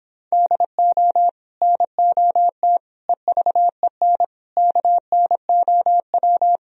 Morse code, heard as a keyed tone, about 26 wpm, 700 Hz. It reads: DO NOT EVEN KNOW